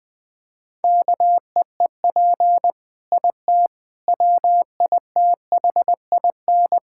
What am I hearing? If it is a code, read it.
Morse code, heard as a keyed tone, about 19 wpm, 700 Hz. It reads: KEEP IT WITHIN
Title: Sentences from Top 1000 Words - Review Part 6 - 20wpm